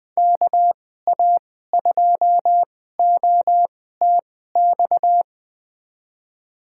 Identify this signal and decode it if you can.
Morse code, heard as a keyed tone, about 21 wpm, spaced as for 15 wpm, 700 Hz. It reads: KA2OTX
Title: USA Call Signs - 6 Characters - 2x Character Spacing - 20wpm